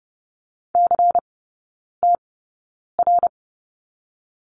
Morse code, heard as a keyed tone, about 30 wpm, spaced as for 9 wpm, 700 Hz. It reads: CTR
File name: Lesson 16e - Letter c - Three Characters 30wpm